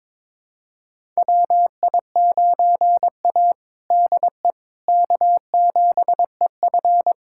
Morse code, heard as a keyed tone, about 21 wpm, 700 Hz. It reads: WI9A DE K7EF
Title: USA Callsign DE Callsign 22wpm